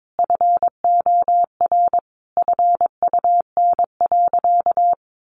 Morse code, heard as a keyed tone, about 22 wpm, 700 Hz. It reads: FOR FUN.
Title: Easy Phrases 22wpm